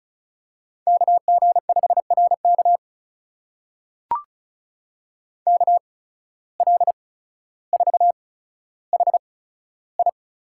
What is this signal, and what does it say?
Morse code, heard as a keyed tone, about 34 wpm, spaced as for 15 wpm, 700 Hz. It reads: KG5RK KL4HI
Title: USA Call Signs - 5 Characters - 8x Character Spacing - 35wpm